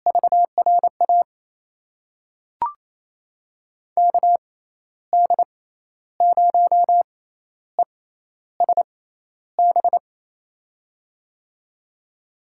Morse code, heard as a keyed tone, about 28 wpm, spaced as for 12 wpm, 700 Hz. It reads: VRA KD0ESB
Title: USA Call Signs - 6 Characters - 6x Character Spacing - 28wpm